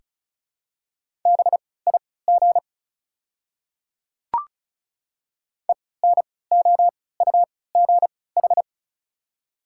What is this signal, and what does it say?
Morse code, heard as a keyed tone, about 36 wpm, spaced as for 20 wpm, 690 Hz. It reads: BIG ENOUGH
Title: 3x Character Spacing - Top 500 Words - 35wpm